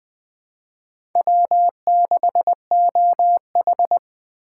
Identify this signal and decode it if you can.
Morse code, heard as a keyed tone, about 21 wpm, 700 Hz. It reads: W6OH
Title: USA Call Signs - 2x Character Spacing - 20wpm